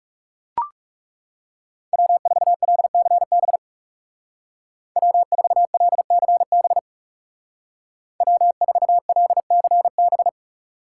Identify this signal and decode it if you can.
Morse code, heard as a keyed tone, about 36 wpm, 700 Hz. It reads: W4LCB W4LCB W4LCB
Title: Speed Racing - USA 1x3 Call Signs 30wpm